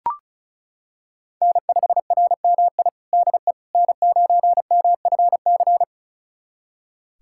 Morse code, heard as a keyed tone, about 36 wpm, 700 Hz. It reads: N5RMI DE N9MFC
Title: USA Callsign DE Callsign - 5 Characters - 35wpm